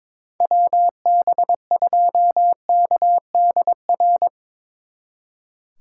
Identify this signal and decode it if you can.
Morse code, heard as a keyed tone, about 25 wpm, 700 Hz. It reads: WB2KDR
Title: Speed Racing - USA 2x3 Call Signs 22wpm